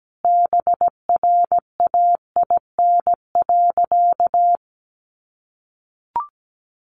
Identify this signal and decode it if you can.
Morse code, heard as a keyed tone, about 17 wpm, 700 Hz. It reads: BRAIN.
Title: Sentences from Top 1100 Words 17wpm